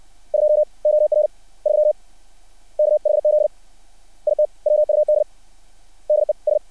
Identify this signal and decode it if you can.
Morse code, heard as a keyed tone, about 60 wpm, 600 Hz. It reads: 2 OR 3 KHZ IT WUD DES
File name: archived / CRSnet